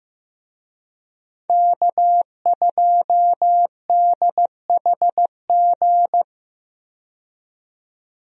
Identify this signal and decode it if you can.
Morse code, heard as a keyed tone, about 16 wpm, 700 Hz. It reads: K2DHG
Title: USA Call Signs - 5 Characters - 15wpm